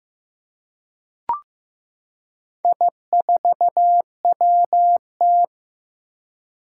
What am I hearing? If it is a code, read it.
Morse code, heard as a keyed tone, about 15 wpm, 700 Hz. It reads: I4WT
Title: Four Challenging Letter-Number Combinations 15wpm